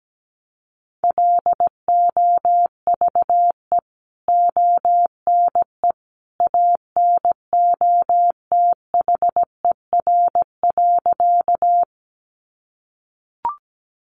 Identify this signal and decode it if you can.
Morse code, heard as a keyed tone, about 17 wpm, 700 Hz. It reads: LOVE ONE ANOTHER.
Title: Sentences from Top 600 Words 17wpm